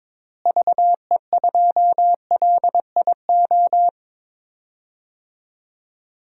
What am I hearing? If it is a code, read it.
Morse code, heard as a keyed tone, about 22 wpm, 700 Hz. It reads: VE2LIO
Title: DXCC-100 Call Signs - 6 Characters - Repeated 2x - 22wpm